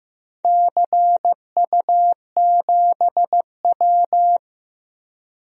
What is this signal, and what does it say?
Morse code, heard as a keyed tone, about 15 wpm, 700 Hz. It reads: CU7W
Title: DXCC-100 Call Signs - 4 Characters - 15wpm